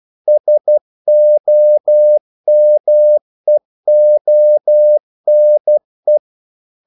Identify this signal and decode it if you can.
Morse code, heard as a keyed tone, about 12 wpm, 600 Hz. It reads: SOMEONE
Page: English quotations in CW (telegraphy - morse)